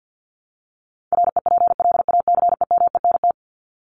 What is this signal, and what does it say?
Morse code, heard as a keyed tone, about 50 wpm, 700 Hz. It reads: REPLACEMENT
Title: Top 1751-2000 Words 50wpm